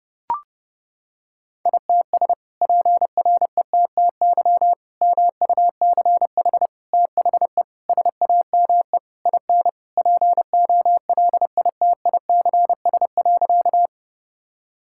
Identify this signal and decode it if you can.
Morse code, heard as a keyed tone, about 30 wpm, 700 Hz. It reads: ITS PRETTY MUCH THE SAME IN POLITICS.
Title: Sentences from Top 1000 Words - Review Part 6 - 30wpm